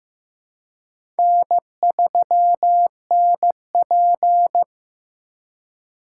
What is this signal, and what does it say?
Morse code, heard as a keyed tone, about 15 wpm, 700 Hz. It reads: N3NP
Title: USA Call Signs - 4 Characters - 2x Character Spacing - 15wpm